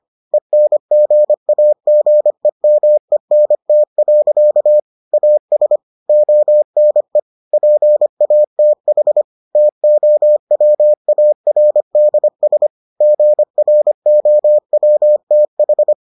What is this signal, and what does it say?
Morse code, heard as a keyed tone, about 25 wpm, 600 Hz. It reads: ENGAGEMENT. AS ONE PATH TOWARDS GROWTH